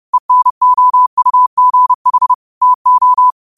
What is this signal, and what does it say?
Morse code, heard as a keyed tone, about 30 wpm, 1000 Hz. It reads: ENOUGH TO
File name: qotdfiles / 30wpm